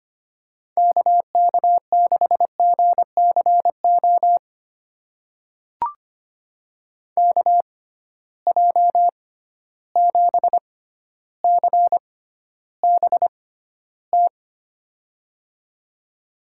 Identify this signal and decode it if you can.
Morse code, heard as a keyed tone, about 25 wpm, spaced as for 13 wpm, 700 Hz. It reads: KK6GCO KJ7CBT
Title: USA Call Signs - 6x Character Spacing - 25wpm